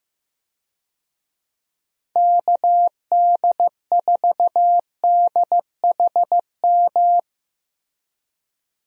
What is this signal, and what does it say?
Morse code, heard as a keyed tone, about 16 wpm, 700 Hz. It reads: KD4DHM
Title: USA Call Signs - 6 Characters - Repeated 2x - 15wpm